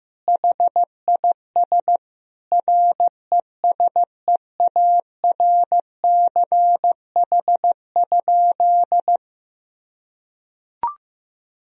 Morse code, heard as a keyed tone, about 16 wpm, 700 Hz. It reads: HIS RESEARCH?
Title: Sentences from Top 1000 Words - Review Part 4 - 15wpm